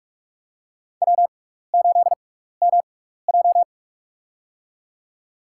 Morse code, heard as a keyed tone, about 42 wpm, spaced as for 15 wpm, 700 Hz. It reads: W8MJ
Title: USA Call Signs - 4 Characters - 6x Character Spacing - 45wpm